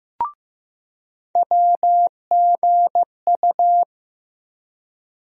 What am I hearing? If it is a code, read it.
Morse code, heard as a keyed tone, about 15 wpm, 700 Hz. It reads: WGU
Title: USA Call Sign Postfixes - 3 Letters 15wpm